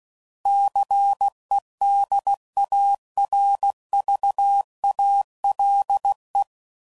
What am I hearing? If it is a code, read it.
Morse code, heard as a keyed tone, about 16 wpm, 800 Hz. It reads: CEDARVALE
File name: CW Code Groups @15WPM / fromSM's / Cities